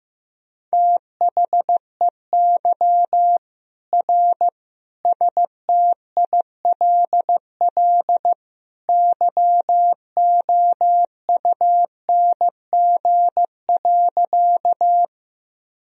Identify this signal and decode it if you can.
Morse code, heard as a keyed tone, about 15 wpm, 700 Hz. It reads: THEY R STILL YOUNG.
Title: Sentences from Top 400 Words and Abbreviations 15wpm